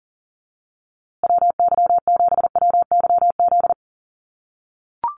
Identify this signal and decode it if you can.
Morse code, heard as a keyed tone, about 40 wpm, 700 Hz. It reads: WY7WYZ
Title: USA Call Signs Repeated 2x 40wpm